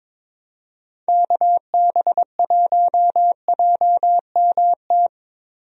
Morse code, heard as a keyed tone, about 22 wpm, 700 Hz. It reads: KB1JMT
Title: USA Call Signs - 6 Characters - Repeated 3x - 22wpm